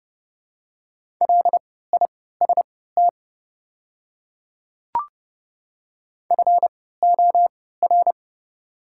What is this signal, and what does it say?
Morse code, heard as a keyed tone, about 30 wpm, spaced as for 17 wpm, 690 Hz. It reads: LIST FOR
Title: 3x Character Spacing - Top 500 Words - 30wpm